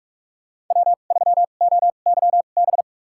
Morse code, heard as a keyed tone, about 46 wpm, spaced as for 32 wpm, 700 Hz. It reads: W3OYB
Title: USA Call Signs - 5 Characters - 2x Character Spacing - 45wpm